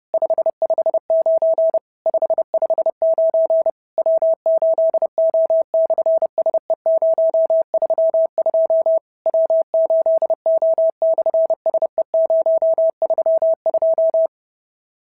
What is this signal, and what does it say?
Morse code, heard as a keyed tone, about 30 wpm, 650 Hz. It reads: 559 559 W8O/SE032 W8O/SE032
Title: SOTA QSO Practice 30wpm